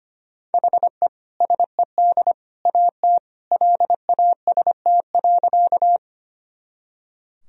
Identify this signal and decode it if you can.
Morse code, heard as a keyed tone, about 25 wpm, 700 Hz. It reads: HE SED AT LAST.
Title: Sentences from Top 300 Words and Abbreviations 25wpm